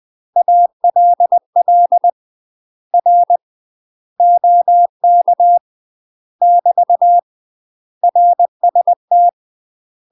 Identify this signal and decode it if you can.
Morse code, heard as a keyed tone, about 20 wpm, 700 Hz. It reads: ALL R OK <BT> RST
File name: UK MASTER QSOs TRAINING / 20WPM / UK MASTER QSOs 0042